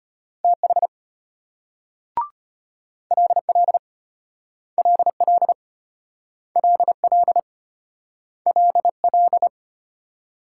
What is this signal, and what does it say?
Morse code, heard as a keyed tone, about 33 wpm, 700 Hz. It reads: TH LL LL LL LL